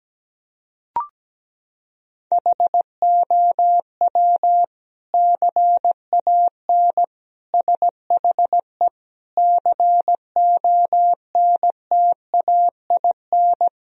Morse code, heard as a keyed tone, about 17 wpm, 700 Hz. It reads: HOW CAN SHE CONTAIN
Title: Sentences from Top 1000 Words - Review Part 2 - 17wpm